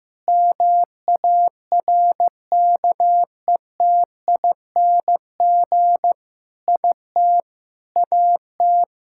Morse code, heard as a keyed tone, about 15 wpm, 700 Hz. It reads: MARKETING IT AT
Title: Sentences from Top 1000 Words - Review Part 5 - 15wpm